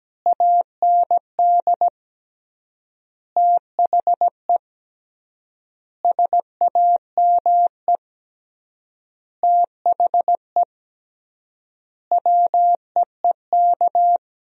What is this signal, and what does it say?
Morse code, heard as a keyed tone, about 17 wpm, 700 Hz. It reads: AND THE SAME THE WEEK
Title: 3x Word-Spacing - Sentences from Top 500 Words - 17wpm